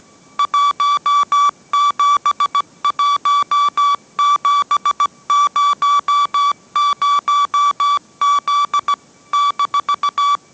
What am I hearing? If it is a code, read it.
Morse code, heard as a keyed tone, about 17 wpm, 1150 Hz. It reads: 171700Z -